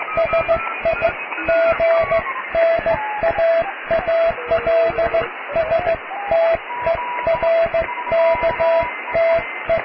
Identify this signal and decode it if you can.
Morse code, heard as a keyed tone, about 15 wpm, 660 Hz, with radio noise. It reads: SIGNAALSTERKTE
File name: cw / tekst(a)+QRM